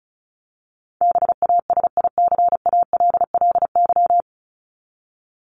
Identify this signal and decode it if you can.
Morse code, heard as a keyed tone, about 34 wpm, 700 Hz. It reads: BASICALLY
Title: Top 1251-1500 Words 35wpm